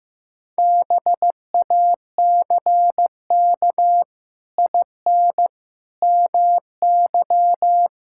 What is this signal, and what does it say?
Morse code, heard as a keyed tone, about 16 wpm, 700 Hz. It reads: BACK IN MY
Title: Pangram Challenge v2 - 15wpm